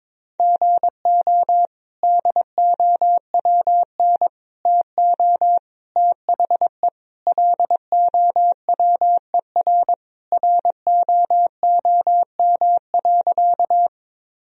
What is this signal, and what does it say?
Morse code, heard as a keyed tone, about 22 wpm, 700 Hz. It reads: GO DOWN TO THE LOWER ROOM.
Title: Sentences from Top 1000 Words - Review Part 4 - 22wpm